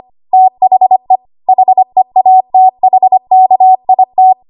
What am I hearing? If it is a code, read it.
Morse code, heard as a keyed tone, about 25 wpm, 750 Hz. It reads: THE HEATHKIT
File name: Archive / 25 WPM / 25WPM